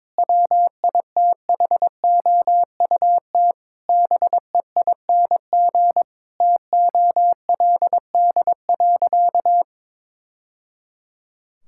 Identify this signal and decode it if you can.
Morse code, heard as a keyed tone, about 22 wpm, 700 Hz. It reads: WITHOUT BEING TOLD.
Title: Sentences from Top 1000 Words - Review Part 6 - 22wpm